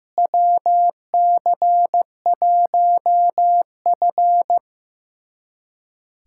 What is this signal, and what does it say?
Morse code, heard as a keyed tone, about 15 wpm, 700 Hz. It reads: WC1F
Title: USA Callsign DE Callsign - 4 Characters - 15wpm